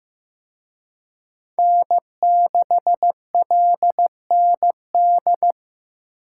Morse code, heard as a keyed tone, about 18 wpm, 700 Hz. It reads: N6LND
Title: Speed Racing - USA 1x3 Call Signs 15wpm